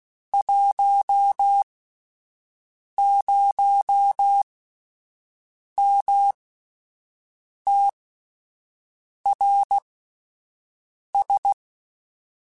Morse code, heard as a keyed tone, about 16 wpm, spaced as for 5 wpm, 800 Hz. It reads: 10MTRS
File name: (spaced) / QSOs Spaced-mp3s / WD0FBT 1 @10 sp